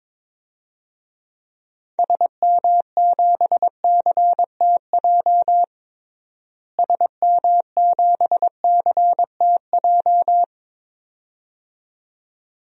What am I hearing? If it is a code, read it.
Morse code, heard as a keyed tone, about 22 wpm, 700 Hz. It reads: SM7CTJ SM7CTJ
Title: DXCC-100 Call Signs - 6 Characters - Repeated 2x - 22wpm